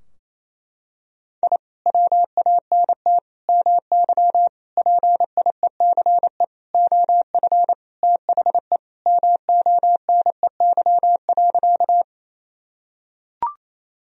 Morse code, heard as a keyed tone, about 28 wpm, 700 Hz. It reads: I WANT MY PIECE OF THE MONEY.
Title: Sentences from Top 1000 Words - Review Part 2 - 28wpm